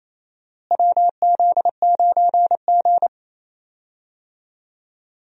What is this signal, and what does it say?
Morse code, heard as a keyed tone, about 28 wpm, 700 Hz. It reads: WZ9G
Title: USA Call Signs - 4 Characters - Repeated 3x - 28wpm